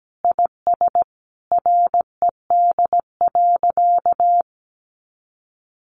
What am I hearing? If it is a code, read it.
Morse code, heard as a keyed tone, about 17 wpm, 700 Hz. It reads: IS RED.
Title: Additional Easy Phrases 17wpm